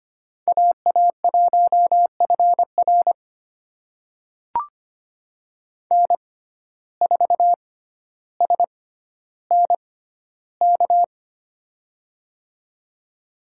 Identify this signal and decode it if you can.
Morse code, heard as a keyed tone, about 25 wpm, spaced as for 13 wpm, 700 Hz. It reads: AA1FR N4SNK
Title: USA Call Signs - 5 Characters - 6x Character Spacing - 25wpm